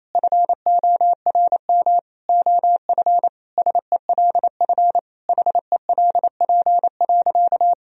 Morse code, heard as a keyed tone, about 28 wpm, 700 Hz. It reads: FORM OF SELF HELP.
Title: Sentences from Top 1000 Words - Review Part 1 - 28wpm